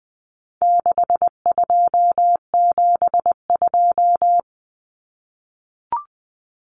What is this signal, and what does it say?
Morse code, heard as a keyed tone, about 20 wpm, 700 Hz. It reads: 6272
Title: Common Cut Numbers - 4-Digit 20wpm